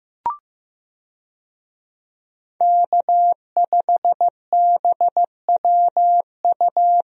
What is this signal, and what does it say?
Morse code, heard as a keyed tone, about 15 wpm, 700 Hz. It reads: K5BWU
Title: USA Call Signs - 5 Characters - Repeated 2x - 15wpm